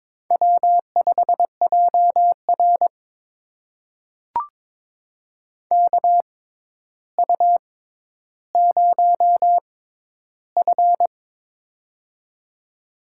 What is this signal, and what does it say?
Morse code, heard as a keyed tone, about 22 wpm, spaced as for 11 wpm, 700 Hz. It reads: W5JR KU0F
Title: USA Call Signs - 4 Characters - 6x Character Spacing - 22wpm